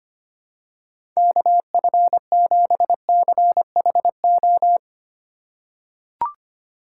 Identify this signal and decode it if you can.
Morse code, heard as a keyed tone, about 25 wpm, 700 Hz. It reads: KF7CHO
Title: USA Call Signs - 6 Characters - 3x Character Spacing - 25wpm